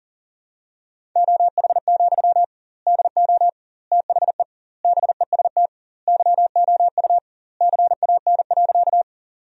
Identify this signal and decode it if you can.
Morse code, heard as a keyed tone, about 40 wpm, 700 Hz. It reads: OH, DO THE BEST YOU CAN.
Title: 2x Word-Spacing - Sentences from Top 500 Words - 40wpm